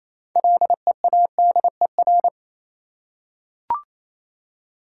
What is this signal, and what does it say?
Morse code, heard as a keyed tone, about 28 wpm, 700 Hz. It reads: LEADER
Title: Six Letter Words - 28wpm